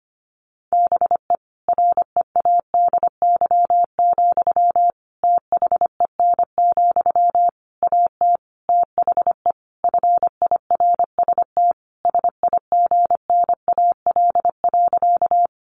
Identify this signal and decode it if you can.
Morse code, heard as a keyed tone, about 25 wpm, 700 Hz. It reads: BE READY, THEN, AT THE FIRST SIGNAL.
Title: Sentences from Top 1400 Words 25wpm